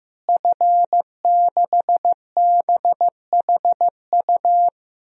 Morse code, heard as a keyed tone, about 15 wpm, 700 Hz. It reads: F6BHU
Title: DXCC-100 Call Signs - 5 Characters - Repeated 2x - 15wpm